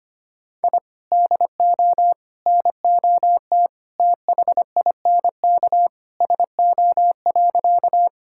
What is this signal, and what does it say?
Morse code, heard as a keyed tone, about 25 wpm, 700 Hz. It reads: I DO NOT THINK SO.
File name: practice / 25wpm / Sentences with 4-Word Phrases - 25wpm